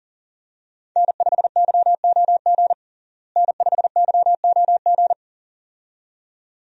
Morse code, heard as a keyed tone, about 40 wpm, 700 Hz. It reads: N5YOG N5YOG
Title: USA Call Signs - 5 Characters - Repeated 2x - 40wpm